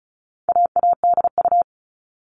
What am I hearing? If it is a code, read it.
Morse code, heard as a keyed tone, about 34 wpm, 700 Hz. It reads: AADU
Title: All Cut Numbers - 4-Digit 35wpm